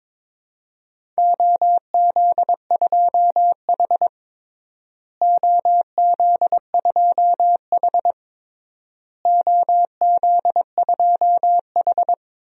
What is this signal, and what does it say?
Morse code, heard as a keyed tone, about 22 wpm, 700 Hz. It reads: OZ2H OZ2H OZ2H
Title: DXCC-100 Call Signs - 4 Characters - Repeated 3x - 22wpm